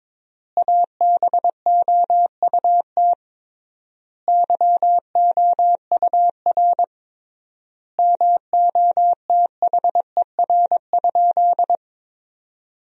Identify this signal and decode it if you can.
Morse code, heard as a keyed tone, about 22 wpm, 700 Hz. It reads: ABOUT YOUR MOTHER?
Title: 3x Word-Spacing - Sentences from Top 200 Words - 22wpm